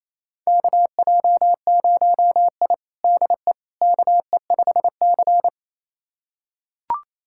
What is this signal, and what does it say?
Morse code, heard as a keyed tone, about 28 wpm, 700 Hz. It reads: KJ0I DE KE5C
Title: USA Callsign DE Callsign - 4 Characters - 28wpm